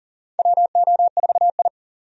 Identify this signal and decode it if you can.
Morse code, heard as a keyed tone, about 39 wpm, 700 Hz. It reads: WO4I
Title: USA Call Signs - 4 Characters - 40wpm